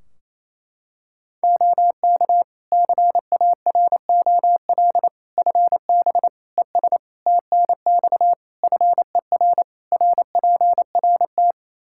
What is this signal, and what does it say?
Morse code, heard as a keyed tone, about 28 wpm, 700 Hz. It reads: OK CAROL FB ES TNX FER RPRT
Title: K1USN SST QSO Practice 28wpm